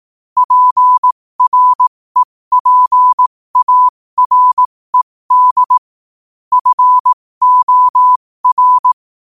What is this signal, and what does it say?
Morse code, heard as a keyed tone, about 18 wpm, spaced as for 15 wpm, 1000 Hz. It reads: PREPARED FOR
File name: qotdfiles / 15wpm